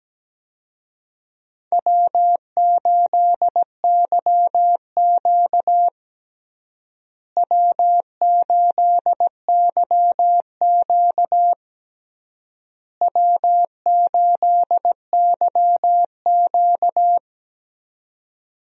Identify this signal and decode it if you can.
Morse code, heard as a keyed tone, about 17 wpm, 700 Hz. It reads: W8YQ W8YQ W8YQ
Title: USA Call Signs - 4 Characters - Repeated 3x - 17wpm